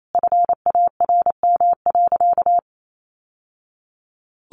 Morse code, heard as a keyed tone, about 28 wpm, 700 Hz. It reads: FARM.
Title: Sentences from Top 1200 Words 28wpm